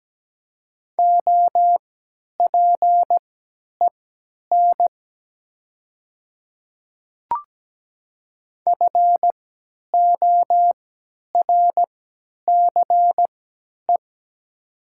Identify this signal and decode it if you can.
Morse code, heard as a keyed tone, about 17 wpm, spaced as for 10 wpm, 700 Hz. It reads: OPEN FORCE
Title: 3x Character Spacing - Top 500 Words - 17wpm